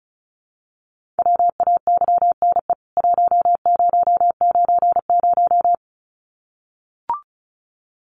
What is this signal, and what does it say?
Morse code, heard as a keyed tone, about 34 wpm, 700 Hz. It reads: WAYNE 1090
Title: CWT Exchange 35wpm